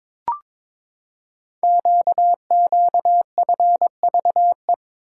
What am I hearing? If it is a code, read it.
Morse code, heard as a keyed tone, about 22 wpm, 700 Hz. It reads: QQFVE
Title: Code Groups - 5 Letter:Numbers - 22wpm